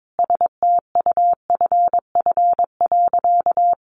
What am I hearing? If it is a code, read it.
Morse code, heard as a keyed tone, about 22 wpm, 700 Hz. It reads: STUFF.
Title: Sentences from Top 1400 Words 22wpm